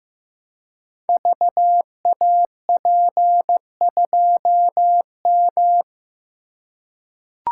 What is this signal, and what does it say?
Morse code, heard as a keyed tone, about 15 wpm, 700 Hz. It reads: VAP2M
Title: Code Groups - 5 Letter:Numbers - 15wpm